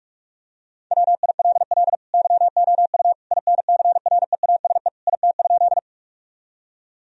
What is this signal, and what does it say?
Morse code, heard as a keyed tone, about 46 wpm, 700 Hz. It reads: WILL YOU INCREASE IT?
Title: Sentences from Top 1000 Words - Review Part 4 - 45wpm